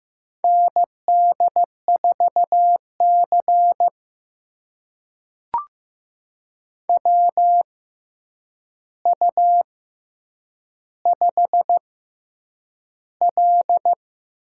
Morse code, heard as a keyed tone, about 15 wpm, spaced as for 8 wpm, 700 Hz. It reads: ND4C WU5L
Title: USA Call Signs - 4 Characters - 6x Character Spacing - 15wpm